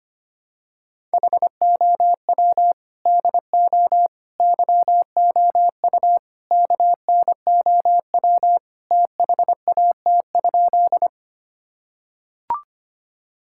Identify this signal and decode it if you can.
Morse code, heard as a keyed tone, about 25 wpm, 700 Hz. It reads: HOW DO YOU KNOW THAT?
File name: practice / 25wpm / Sentences with 4-Word Phrases - 25wpm